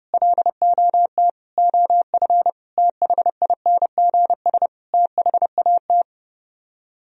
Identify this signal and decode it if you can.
Morse code, heard as a keyed tone, about 30 wpm, 700 Hz. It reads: LOT OF THINGS THAT
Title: Speed-Racing - Sets of 5 Words 25wpm